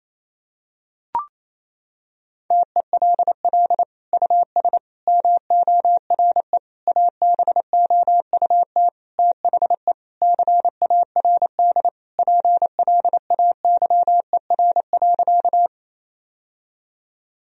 Morse code, heard as a keyed tone, about 28 wpm, 700 Hz. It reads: TELL US MORE ABOUT THE CARD PLAYER.
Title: Sentences from Top 1000 Words - Review Part 4 - 28wpm